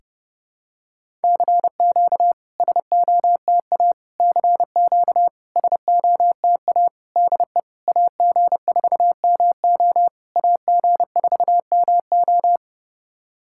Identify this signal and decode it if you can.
Morse code, heard as a keyed tone, about 30 wpm, 700 Hz. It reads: CQ SOTA CQ SOTA DE AG4MO AG4MO
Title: SOTA QSO Practice 30wpm